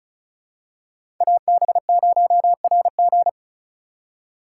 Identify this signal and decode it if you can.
Morse code, heard as a keyed tone, about 36 wpm, 700 Hz. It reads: AB0RG
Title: USA Call Signs - 5 Characters - 3x Character Spacing - 35wpm